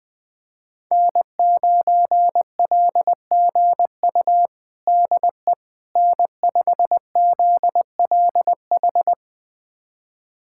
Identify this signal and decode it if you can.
Morse code, heard as a keyed tone, about 21 wpm, 700 Hz. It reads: N9LGU DE N5ZLH
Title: USA Callsign DE Callsign - 5 Characters - 20wpm